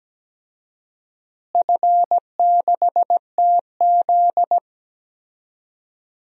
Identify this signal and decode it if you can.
Morse code, heard as a keyed tone, about 17 wpm, 700 Hz. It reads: F6TZ
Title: DXCC-100 Call Signs - 4 Characters - Repeated 2x - 17wpm